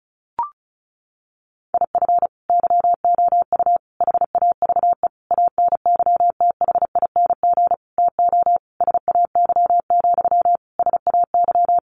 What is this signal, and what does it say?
Morse code, heard as a keyed tone, about 34 wpm, 700 Hz. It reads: IF YOU HAVE ANYTHING TO SAY, SAY
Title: Sentences from Top 1000 Words 35wpm